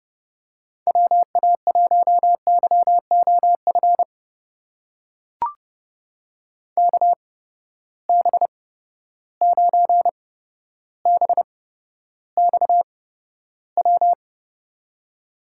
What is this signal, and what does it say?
Morse code, heard as a keyed tone, about 30 wpm, spaced as for 13 wpm, 700 Hz. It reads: WA1YOF KB9BXW
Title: USA Call Signs - 6 Characters - 8x Character Spacing - 30wpm